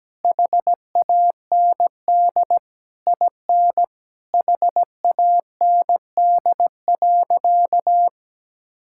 Morse code, heard as a keyed tone, about 17 wpm, 700 Hz. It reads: HAND IN HAND.
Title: Sentences from Top 1000 Words - Review Part 5 - 17wpm